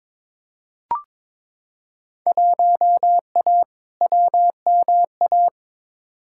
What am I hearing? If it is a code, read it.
Morse code, heard as a keyed tone, about 22 wpm, 700 Hz. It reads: 1A WMA
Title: Field Day Exchanges - 22wpm